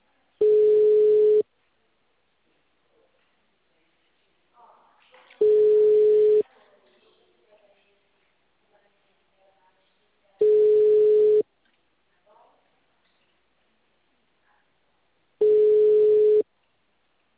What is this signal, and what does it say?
Ringback tone (Europe, 425 Hz).
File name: out- -2011- - - .0